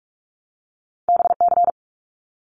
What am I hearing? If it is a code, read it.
Morse code, heard as a keyed tone, about 46 wpm, 700 Hz. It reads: BC
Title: Canadian Provinces and Territories 45wpm